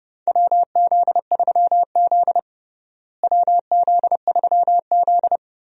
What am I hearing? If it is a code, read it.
Morse code, heard as a keyed tone, about 32 wpm, 700 Hz. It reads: WZ3Z WZ3Z
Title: USA Call Signs - 4 Characters - Repeated 2x - 30wpm